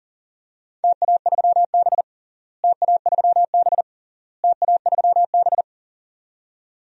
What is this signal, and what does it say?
Morse code, heard as a keyed tone, about 40 wpm, 700 Hz. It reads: TA3B TA3B TA3B
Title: DXCC-100 Call Signs - 4 Characters - Repeated 3x - 40wpm